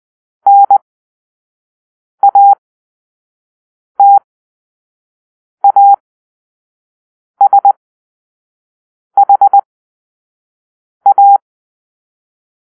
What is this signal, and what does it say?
Morse code, heard as a keyed tone, about 20 wpm, spaced as for 5 wpm, 800 Hz. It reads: NATASHA
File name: arrl / Grp-2 Drill-3